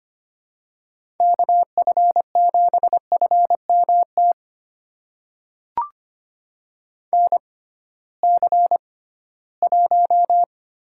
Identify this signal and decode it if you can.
Morse code, heard as a keyed tone, about 25 wpm, spaced as for 16 wpm, 700 Hz. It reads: KF7FMT NC1
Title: USA Call Signs - 6x Character Spacing - 25wpm